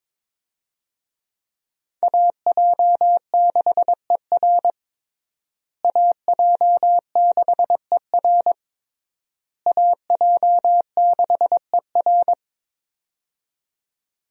Morse code, heard as a keyed tone, about 22 wpm, 700 Hz. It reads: AJ6ER AJ6ER AJ6ER
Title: USA Call Signs - 5 Characters - Repeated 3x - 22wpm